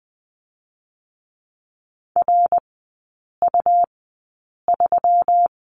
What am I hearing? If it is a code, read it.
Morse code, heard as a keyed tone, about 20 wpm, spaced as for 8 wpm, 700 Hz. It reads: RU3
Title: 184 Lesson 30e - Letter b - Three Characters 20wpm